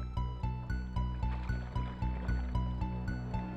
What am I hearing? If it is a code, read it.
There is no code.